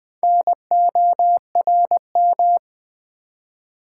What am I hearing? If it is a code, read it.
Morse code, heard as a keyed tone, about 20 wpm, 700 Hz. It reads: NORM
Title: Build A Word - 2 Letters At A Time - 4-Letter Words - 20wpm